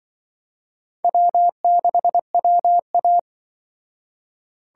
Morse code, heard as a keyed tone, about 24 wpm, 700 Hz. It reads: W6WA
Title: Speed Racing - USA 1x2 Call Signs 20wpm